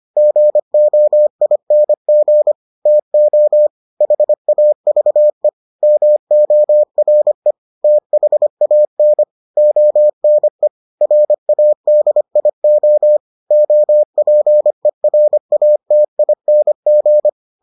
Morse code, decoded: GOING TO HAVE MORE THAN ONE RADIO OPERATING

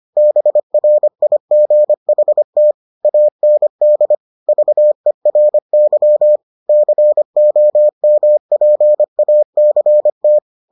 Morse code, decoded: BRIGHT AND VERY COMPACT

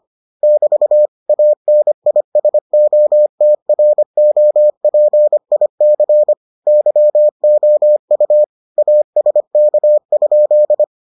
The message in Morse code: <BT> ANISOTROPIC YOU ASK?

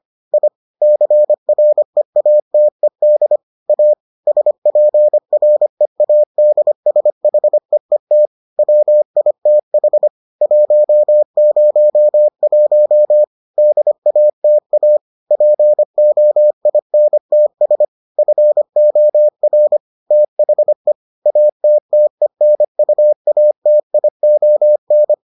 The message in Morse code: I CREATED A SPREADSHEET WITH 101 DATA POINTS FOR THE ATTENUATION

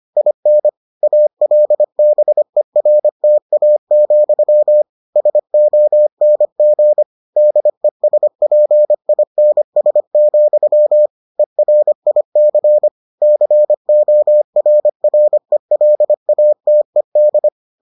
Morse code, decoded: IN ALBERTA, SONG DESPINS, ERIC CORRELATED

